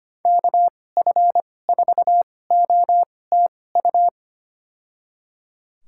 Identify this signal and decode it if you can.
Morse code, heard as a keyed tone, about 25 wpm, spaced as for 18 wpm, 700 Hz. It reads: KF4OTU
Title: USA Call Signs - 6 Characters - 2x Character Spacing - 25wpm